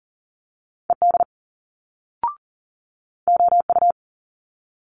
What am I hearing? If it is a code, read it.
Morse code, heard as a keyed tone, about 39 wpm, 700 Hz. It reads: ED OU
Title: Multi-letter Phonemes Quick Fire 40wpm